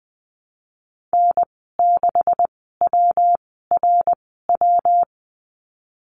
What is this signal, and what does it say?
Morse code, heard as a keyed tone, about 19 wpm, spaced as for 14 wpm, 700 Hz. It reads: N6WRW
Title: 140 Lesson 23g - Letter f - Call Signs 20wpm